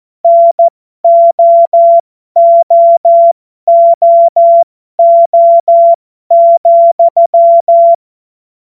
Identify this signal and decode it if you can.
Morse code, heard as a keyed tone, about 14 wpm, spaced as for 12 wpm, 680 Hz. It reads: NOOOO,